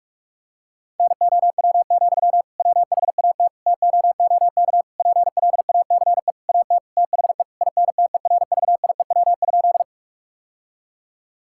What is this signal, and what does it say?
Morse code, heard as a keyed tone, about 46 wpm, 700 Hz. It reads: NOW, WHAT TOOK PLACE AT THE INTERVIEW?
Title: Sentences from Top 1600 Words 45wpm